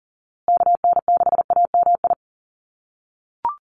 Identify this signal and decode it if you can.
Morse code, heard as a keyed tone, about 40 wpm, 700 Hz. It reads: KN6AMI
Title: USA Call Signs Repeated 3x 40wpm